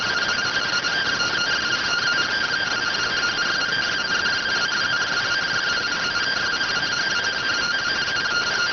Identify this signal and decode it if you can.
Two-tone teleprinter signal, about 170 Hz shift.